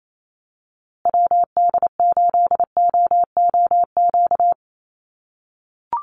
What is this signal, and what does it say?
Morse code, heard as a keyed tone, about 28 wpm, 700 Hz. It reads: WD8OOQ
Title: USA Call Signs Repeated 2x 28wpm